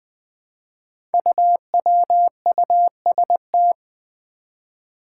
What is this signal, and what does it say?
Morse code, heard as a keyed tone, about 20 wpm, 700 Hz. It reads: UWUST
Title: Five Challenging Letter-Number Combinations 20wpm